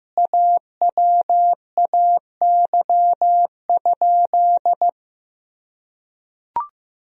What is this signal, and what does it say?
Morse code, heard as a keyed tone, about 15 wpm, 700 Hz. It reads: AWAY?
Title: Sentences from Top 1000 Words - Review Part 1 - 15wpm